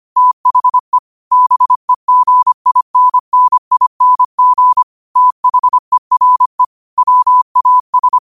Morse code, heard as a keyed tone, about 25 wpm, 1000 Hz. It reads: THE BEGINNING THERE WAS